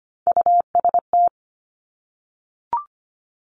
Morse code, heard as a keyed tone, about 25 wpm, 700 Hz. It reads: UST
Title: Top Three Letter Combinations in Words 25wpm